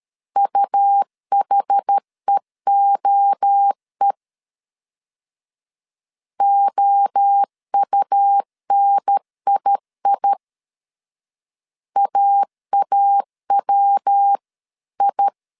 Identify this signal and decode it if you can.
Morse code, heard as a keyed tone, about 13 wpm, 800 Hz. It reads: UHEOE OUNII AAWI
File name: CW / DET / 1 3 1500C S C 12wpm 2750